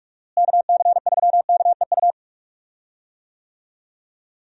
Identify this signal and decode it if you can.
Morse code, heard as a keyed tone, about 44 wpm, 700 Hz. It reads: KC3KEU
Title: USA Call Signs - 6 Characters - Repeated 3x - 45wpm